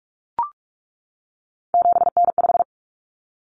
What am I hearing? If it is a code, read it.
Morse code, heard as a keyed tone, about 44 wpm, 700 Hz. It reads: 7N5
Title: Common Cut Numbers - 3-Digit 45wpm